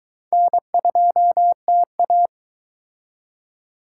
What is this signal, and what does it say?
Morse code, heard as a keyed tone, about 22 wpm, 700 Hz. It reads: N2TA